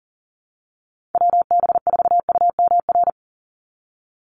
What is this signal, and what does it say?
Morse code, heard as a keyed tone, about 40 wpm, 700 Hz. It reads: WB4UMR
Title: USA Call Signs 40wpm